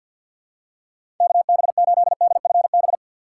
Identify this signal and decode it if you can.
Morse code, heard as a keyed tone, about 50 wpm, 700 Hz. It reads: KB8DFB
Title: USA Call Signs - 6 Characters - 4x Character Spacing - 50wpm